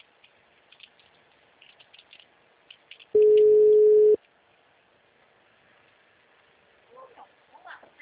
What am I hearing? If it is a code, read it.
Ringback tone (Europe, 425 Hz), with speech.
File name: out- -2012- - - .9901